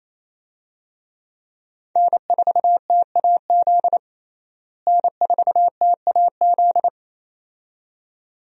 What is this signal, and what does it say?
Morse code, heard as a keyed tone, about 27 wpm, 700 Hz. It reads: N4TAZ N4TAZ
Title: USA Call Signs - 5 Characters - Repeated 2x - 28wpm